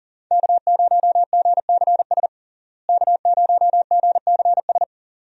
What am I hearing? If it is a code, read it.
Morse code, heard as a keyed tone, about 39 wpm, 700 Hz. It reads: K0GCS K0GCS
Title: USA Call Signs - 5 Characters - Repeated 2x - 40wpm